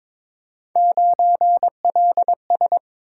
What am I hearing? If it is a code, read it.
Morse code, heard as a keyed tone, about 22 wpm, 700 Hz. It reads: 9LS
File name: practice / 22wpm / Three Letter-Number Combinations - 22wpm